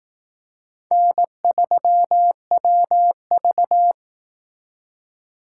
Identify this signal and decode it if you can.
Morse code, heard as a keyed tone, about 18 wpm, 700 Hz. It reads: N3WV
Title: Speed Racing - USA 1x2 Call Signs 15wpm